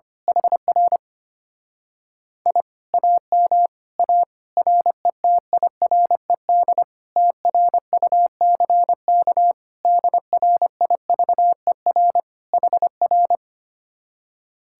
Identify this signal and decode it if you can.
Morse code, heard as a keyed tone, about 24 wpm, 700 Hz. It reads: HR I AM A RETIRED TRUCK DRIVER HR
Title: Speed-Racing Common QSO Phrases - 25wpm